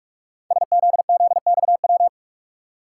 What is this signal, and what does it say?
Morse code, heard as a keyed tone, about 44 wpm, 700 Hz. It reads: SZZXW
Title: Code Groups - 5 Letters - 45wpm